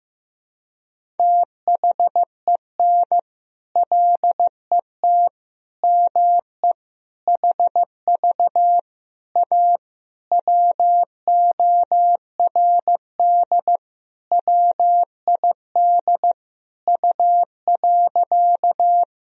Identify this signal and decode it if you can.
Morse code, heard as a keyed tone, about 15 wpm, 700 Hz. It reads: THEN LET ME HV A WORD WID U.
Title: Sentences from Top 300 Words and Abbreviations 15wpm